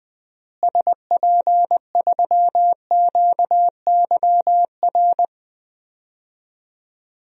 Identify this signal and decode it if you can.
Morse code, heard as a keyed tone, about 20 wpm, 700 Hz. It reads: SP3QYR
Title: DXCC-100 Call Signs - 6 Characters - Repeated 3x - 20wpm